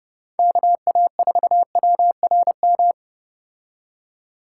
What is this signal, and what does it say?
Morse code, heard as a keyed tone, about 30 wpm, 700 Hz. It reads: KA4WRM